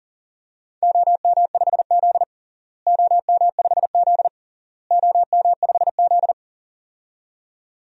Morse code, heard as a keyed tone, about 37 wpm, 700 Hz. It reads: OM5Z OM5Z OM5Z
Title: DXCC-100 Call Signs - 4 Characters - Repeated 3x - 40wpm